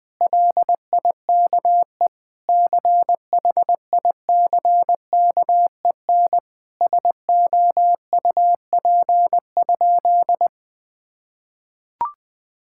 Morse code, heard as a keyed tone, about 20 wpm, 700 Hz. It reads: LIKE CHICKEN SOUP?
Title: Sentences from Top 1000 Words - Review Part 6 - 20wpm